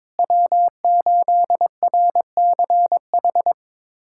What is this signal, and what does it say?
Morse code, heard as a keyed tone, about 22 wpm, 700 Hz. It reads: W8RCH
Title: USA Callsign DE Callsign - 5 Characters - 22wpm